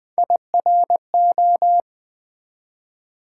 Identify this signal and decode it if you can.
Morse code, heard as a keyed tone, about 20 wpm, 690 Hz. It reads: IRO
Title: Build A Word - 1 Letter At A Time - 5-Letter Words - 20wpm